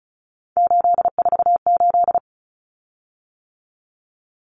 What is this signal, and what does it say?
Morse code, heard as a keyed tone, about 34 wpm, 700 Hz. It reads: 848
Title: Common Cut Numbers - 3-Digit 35wpm